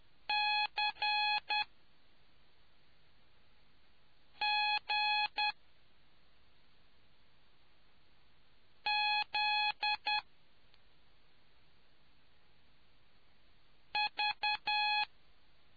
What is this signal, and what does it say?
Morse code, heard as a keyed tone, about 10 wpm, 840 Hz. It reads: C G Z V